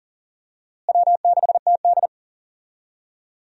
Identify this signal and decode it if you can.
Morse code, heard as a keyed tone, about 40 wpm, 700 Hz. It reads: W6TD
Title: USA Call Signs - 4 Characters - Repeated 2x - 40wpm